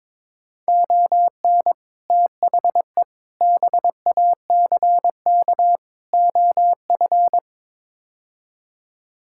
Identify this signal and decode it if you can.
Morse code, heard as a keyed tone, about 22 wpm, 700 Hz. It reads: ON THE BACK OF
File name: practice / 22wpm / Build A Phrase - 5-Word Phrases - 22wpm